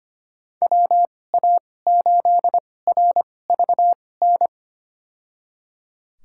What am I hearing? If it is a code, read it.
Morse code, heard as a keyed tone, about 25 wpm, spaced as for 18 wpm, 700 Hz. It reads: WA8RVN
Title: USA Call Signs - 2x Character Spacing - 25wpm